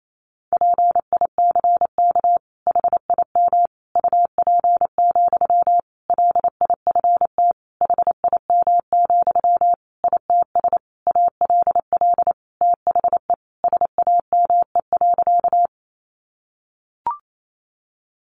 Morse code, heard as a keyed tone, about 28 wpm, 700 Hz. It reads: PICK HIM UP, LIFT HIM, ITS ALL THE SAME.
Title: Sentences from Top 1000 Words 28wpm